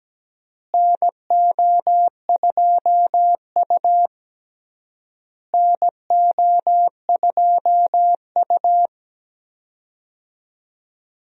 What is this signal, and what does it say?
Morse code, heard as a keyed tone, about 17 wpm, 700 Hz. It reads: NO2U NO2U
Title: USA Call Signs - 4 Characters - Repeated 3x - 17wpm